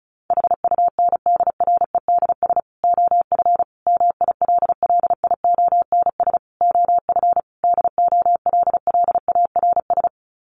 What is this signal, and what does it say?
Morse code, heard as a keyed tone, about 34 wpm, 700 Hz. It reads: HUNDREDS OF MILLIONS OF DOLLARS
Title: Sets of 5 Words 35wpm